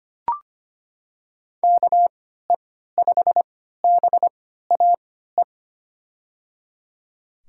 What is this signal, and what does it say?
Morse code, heard as a keyed tone, about 25 wpm, spaced as for 14 wpm, 700 Hz. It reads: KE5BAE
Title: USA Call Signs - 3x Character Spacing - 25wpm